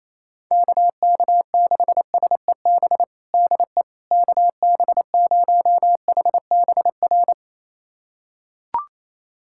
Morse code, heard as a keyed tone, about 27 wpm, 700 Hz. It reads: KK6SEB DE KB0HBR
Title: USA Callsign DE Callsign - 6 Characters - 28wpm